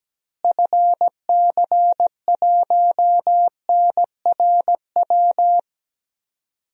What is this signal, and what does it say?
Morse code, heard as a keyed tone, about 17 wpm, 700 Hz. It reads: FC1NRW
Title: DXCC-100 Call Signs - 6 Characters - Repeated 3x - 17wpm